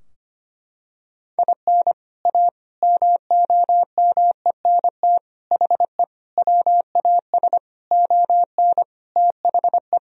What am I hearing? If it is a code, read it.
Morse code, heard as a keyed tone, about 25 wpm, 700 Hz. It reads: IN A MOMENT HE WAS ON THE
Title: Sentences from Top 1000 Words - Review Part 2 - 25wpm